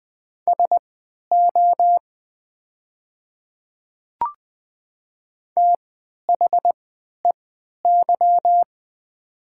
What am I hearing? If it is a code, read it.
Morse code, heard as a keyed tone, about 20 wpm, spaced as for 11 wpm, 700 Hz. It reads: SO THEY
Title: 3x Character Spacing - Top 100 Words - 20wpm